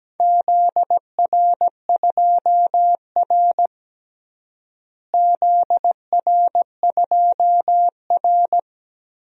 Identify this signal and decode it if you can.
Morse code, heard as a keyed tone, about 17 wpm, 700 Hz. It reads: ZR2R ZR2R
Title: DXCC-100 Call Signs - 4 Characters - Repeated 3x - 17wpm